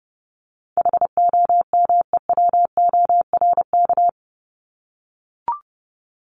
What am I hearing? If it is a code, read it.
Morse code, heard as a keyed tone, about 30 wpm, 700 Hz. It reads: HOMEWORK
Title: Top 1501-1750 Words 30wpm